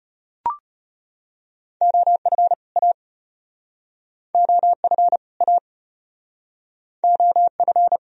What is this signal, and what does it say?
Morse code, heard as a keyed tone, about 34 wpm, 700 Hz. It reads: OF A OF A OF